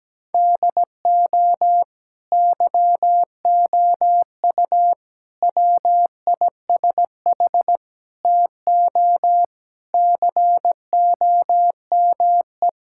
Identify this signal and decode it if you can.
Morse code, heard as a keyed tone, about 17 wpm, 700 Hz. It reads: DO YOU WISH TO COME
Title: Sentences from Top 1600 Words 17wpm